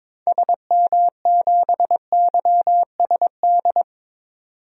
Morse code, heard as a keyed tone, about 22 wpm, 700 Hz. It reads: SM7YSD
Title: DXCC-100 Call Signs - 6 Characters - 22wpm